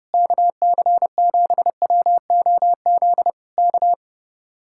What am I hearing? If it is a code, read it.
Morse code, heard as a keyed tone, about 31 wpm, 700 Hz. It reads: KC7WOZ K